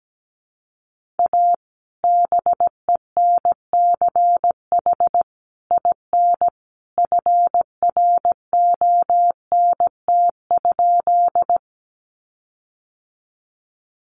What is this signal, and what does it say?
Morse code, heard as a keyed tone, about 17 wpm, 700 Hz. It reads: A BENCH IN FRONT?
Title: Sentences from Top 1400 Words 17wpm